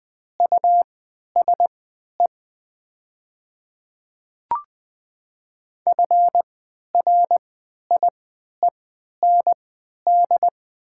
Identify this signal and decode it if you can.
Morse code, heard as a keyed tone, about 20 wpm, spaced as for 11 wpm, 700 Hz. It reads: USE FRIEND
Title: 3x Character Spacing - Top 300 Words - 20wpm